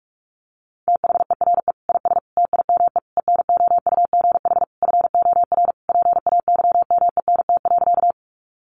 Morse code, heard as a keyed tone, about 44 wpm, 700 Hz. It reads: THERE IS TIME ENOUGH FOR PAYMENT.
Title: Sentences from Top 1200 Words 45wpm